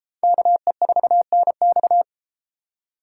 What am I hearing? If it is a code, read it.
Morse code, heard as a keyed tone, about 32 wpm, 700 Hz. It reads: KE4NX